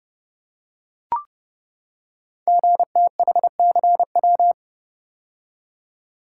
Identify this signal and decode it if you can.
Morse code, heard as a keyed tone, about 28 wpm, 700 Hz. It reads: GTHCW
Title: Code Groups - 5 Letters - 30wpm